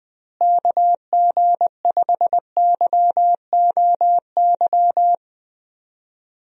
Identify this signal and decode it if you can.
Morse code, heard as a keyed tone, about 19 wpm, 700 Hz. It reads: KG5YOY
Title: USA Call Signs - 6 Characters - Repeated 3x - 20wpm